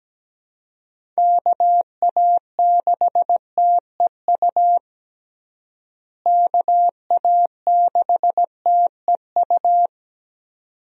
Morse code, heard as a keyed tone, about 17 wpm, 700 Hz. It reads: KA6TEU KA6TEU
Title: USA Call Signs - 6 Characters - Repeated 2x - 17wpm